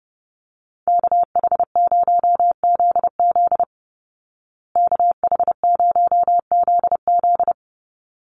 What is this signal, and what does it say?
Morse code, heard as a keyed tone, about 30 wpm, 700 Hz. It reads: KH0ZZ KH0ZZ
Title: Call Signs Repeated 2x - 30wpm